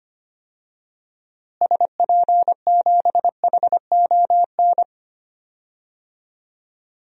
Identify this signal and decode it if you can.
Morse code, heard as a keyed tone, about 25 wpm, 700 Hz. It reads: SP7HON
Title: DXCC-100 Call Signs - 6 Characters - Repeated 2x - 25wpm